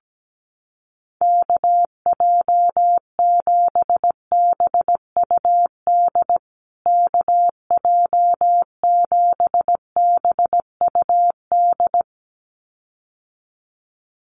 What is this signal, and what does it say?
Morse code, heard as a keyed tone, about 17 wpm, 700 Hz. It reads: KJ7BUD KJ7BUD
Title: USA Call Signs Repeated 2x 17wpm